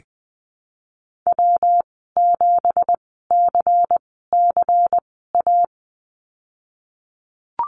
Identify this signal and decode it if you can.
Morse code, heard as a keyed tone, about 20 wpm, spaced as for 15 wpm, 700 Hz. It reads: W7CCA
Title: 173 Lesson 28g - Number 9 - Call Signs 20wpm